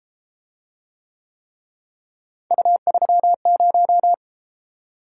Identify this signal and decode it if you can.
Morse code, heard as a keyed tone, about 33 wpm, 700 Hz. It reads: U30